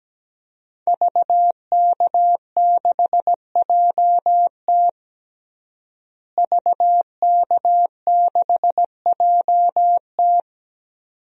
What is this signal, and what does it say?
Morse code, heard as a keyed tone, about 17 wpm, 700 Hz. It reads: VK6JT VK6JT
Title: DXCC-100 Call Signs - 5 Characters - Repeated 2x - 17wpm